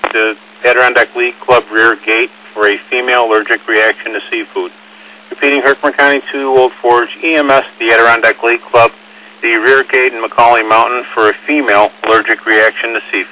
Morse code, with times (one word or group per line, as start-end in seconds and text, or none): none